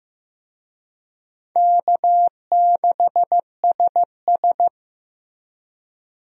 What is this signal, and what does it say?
Morse code, heard as a keyed tone, about 16 wpm, 700 Hz. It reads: K6SS
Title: USA Call Signs - 4 Characters - Repeated 2x - 15wpm